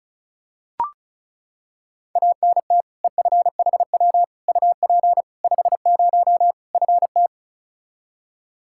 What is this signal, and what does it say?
Morse code, heard as a keyed tone, about 34 wpm, 700 Hz. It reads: ANT EFHW UP 50 FT
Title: Practice QSO - Exchange 2 - Info - 35wpm